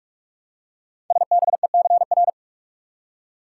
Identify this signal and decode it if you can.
Morse code, heard as a keyed tone, about 44 wpm, 700 Hz. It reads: SBECR
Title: Code Groups - 5 Letters - 45wpm